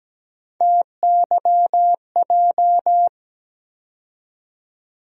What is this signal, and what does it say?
Morse code, heard as a keyed tone, about 17 wpm, 700 Hz. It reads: TYJ
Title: USA Call Sign Postfixes - 3 Letters 17wpm